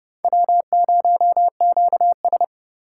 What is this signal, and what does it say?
Morse code, heard as a keyed tone, about 28 wpm, 700 Hz. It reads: W0QS